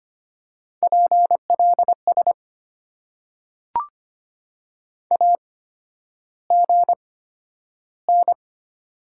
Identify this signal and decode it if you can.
Morse code, heard as a keyed tone, about 25 wpm, spaced as for 11 wpm, 690 Hz. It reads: PLS AGN
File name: Top 90 QSO Elements - 8x Character Spacing - 25wpm